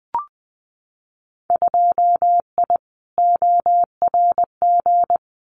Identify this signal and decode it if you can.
Morse code, heard as a keyed tone, about 20 wpm, 700 Hz. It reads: 2I ORG
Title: Winter Field Day Exchanges Encore 20wpm